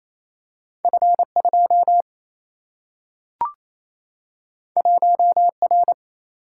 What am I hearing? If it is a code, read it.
Morse code, heard as a keyed tone, about 28 wpm, 700 Hz. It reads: F 2 1 R
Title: Single-Letter Single-Number 28wpm